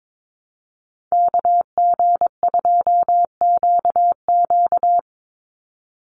Morse code, heard as a keyed tone, about 22 wpm, 700 Hz. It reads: KG2QQ
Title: USA Call Signs 22wpm